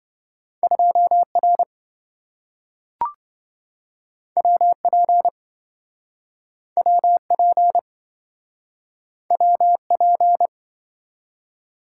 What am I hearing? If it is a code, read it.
Morse code, heard as a keyed tone, about 27 wpm, 700 Hz. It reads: 2R WP WP WP